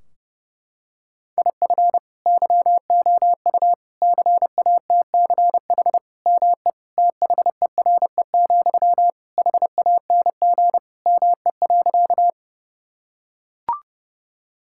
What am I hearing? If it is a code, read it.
Morse code, heard as a keyed tone, about 30 wpm, 700 Hz. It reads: IF YOU CATCH ME THERE, HANG ME.
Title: Sentences from Top 1600 Words 30wpm